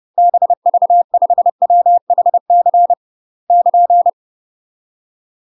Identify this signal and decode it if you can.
Morse code, heard as a keyed tone, about 30 wpm, 700 Hz. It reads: BV5WHC <KN>
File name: UK MASTER QSOs TRAINING / 30WPM / UK MASTER QSOs 0043